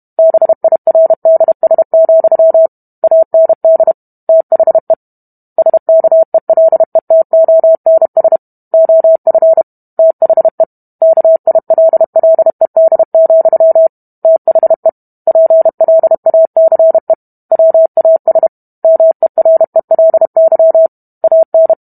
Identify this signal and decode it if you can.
Morse code, heard as a keyed tone, about 32 wpm, 650 Hz. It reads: BIRDS, AND THE SKELETONS OF THE KILLED, THE PLACE WAS MERELY AN